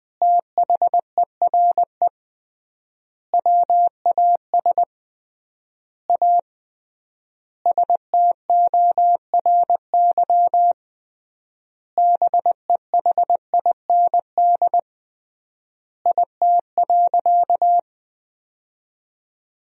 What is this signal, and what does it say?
Morse code, heard as a keyed tone, about 20 wpm, 700 Hz. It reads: THERE WAS A STORY BEHIND IT.
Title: 3x Word-Spacing - Sentences from Top 500 Words - 20wpm